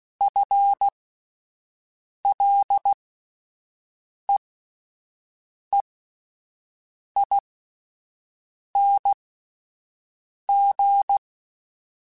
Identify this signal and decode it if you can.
Morse code, heard as a keyed tone, about 16 wpm, spaced as for 5 wpm, 800 Hz. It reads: FLEEING
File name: Spaced / Chuckles Spaced-mp3s / Temptation sp @10